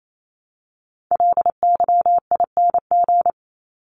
Morse code, heard as a keyed tone, about 28 wpm, 700 Hz. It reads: LYING
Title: ING Suffix - Top 100 Words with Base Words 28wpm